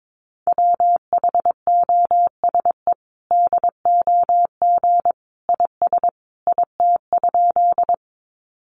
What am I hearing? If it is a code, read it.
Morse code, heard as a keyed tone, about 22 wpm, 700 Hz. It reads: WHOSE DOG IS IT?
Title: Sentences from Top 1200 Words 22wpm